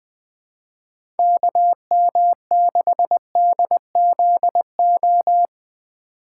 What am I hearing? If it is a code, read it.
Morse code, heard as a keyed tone, about 20 wpm, 700 Hz. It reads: KM6DZO
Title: USA Call Signs - 6 Characters - 4x Character Spacing - 20wpm